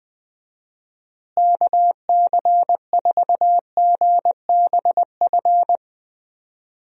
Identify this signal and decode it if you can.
Morse code, heard as a keyed tone, about 20 wpm, 700 Hz. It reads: KC4GBF
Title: USA Call Signs - 6 Characters - Repeated 3x - 20wpm